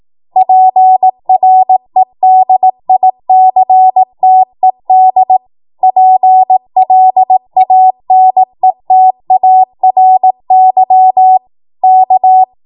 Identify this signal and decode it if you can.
Morse code, heard as a keyed tone, about 18 wpm, 750 Hz. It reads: PREDICTED PLANETARY K
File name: Morse / 18WPM